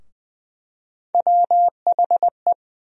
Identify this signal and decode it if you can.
Morse code, heard as a keyed tone, about 20 wpm, 700 Hz. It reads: WHE
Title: USA Call Sign Postfixes - 3 Letters 20wpm